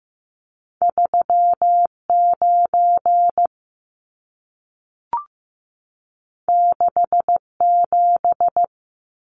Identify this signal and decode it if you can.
Morse code, heard as a keyed tone, about 15 wpm, 700 Hz. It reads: 39 67
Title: Numbers - 2-Digit 15wpm